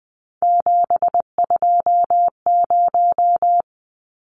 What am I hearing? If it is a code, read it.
Morse code, heard as a keyed tone, about 20 wpm, 700 Hz. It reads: 720
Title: Numbers - 3-Digit 20wpm